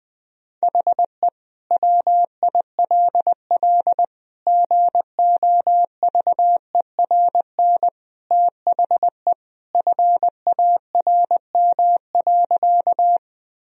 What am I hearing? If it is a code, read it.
Morse code, heard as a keyed tone, about 21 wpm, 700 Hz. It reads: HE WILL GOVERN THE FARM.
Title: Sentences from Top 1000 Words - Review Part 2 - 20wpm